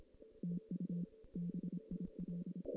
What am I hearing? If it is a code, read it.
Morse code, heard as a keyed tone, about 26 wpm, 180 Hz. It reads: TU BIL
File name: 2016 JAN CW NAQP / 2016 Jan naqp k5ot - 0-6